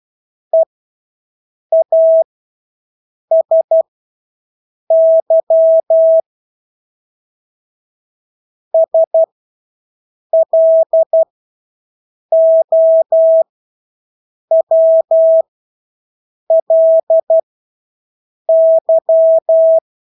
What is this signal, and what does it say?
Morse code, heard as a keyed tone, about 12 wpm, spaced as for 6 wpm, 650 Hz. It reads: EASY SLOWLY
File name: Engelske ord 12-6 wpm